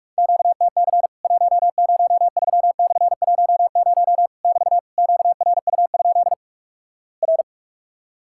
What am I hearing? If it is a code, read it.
Morse code, heard as a keyed tone, about 46 wpm, 700 Hz. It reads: QTC 103/10 <BT> QRU? R
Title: Worked All Europe DX Contest - QTC Exchange - 45wpm